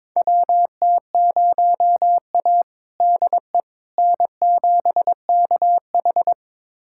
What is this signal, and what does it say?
Morse code, heard as a keyed tone, about 22 wpm, 700 Hz. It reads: WT0A DE N7KH
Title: USA Callsign DE Callsign - 4 Characters - 22wpm